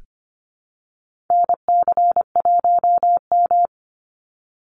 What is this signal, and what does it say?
Morse code, heard as a keyed tone, about 25 wpm, 700 Hz. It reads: NC1M
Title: Call Signs Repeated 2x - 25wpm